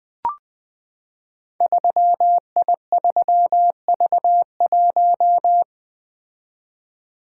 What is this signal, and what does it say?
Morse code, heard as a keyed tone, about 20 wpm, 700 Hz. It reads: 3I3V1
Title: Code Groups - 5 Letter:Numbers - 20wpm